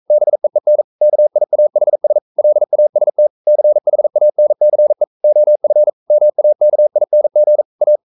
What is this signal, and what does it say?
Morse code, heard as a keyed tone, about 42 wpm, 600 Hz. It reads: BEEN KIAHS LAST CHANCE OF MAKING A
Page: Romanen Never af Ken Follet på engelsk.